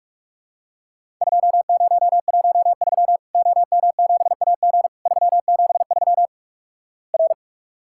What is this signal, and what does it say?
Morse code, heard as a keyed tone, about 46 wpm, 700 Hz. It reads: 2013 OM7AG 373 R
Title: Worked All Europe DX Contest - QTC Exchange - 45wpm